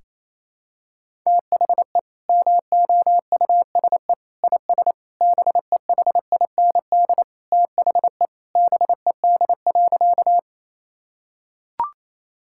Morse code, heard as a keyed tone, about 28 wpm, 700 Hz. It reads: THE MOUSE IS BEHIND THE BED.
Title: Sentences from Top 1000 Words - Review Part 5 - 28wpm